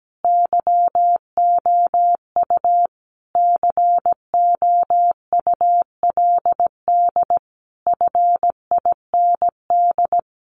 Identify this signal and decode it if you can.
Morse code, heard as a keyed tone, about 17 wpm, 700 Hz. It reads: YOU COULD FIND
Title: Sentences from Top 1400 Words 17wpm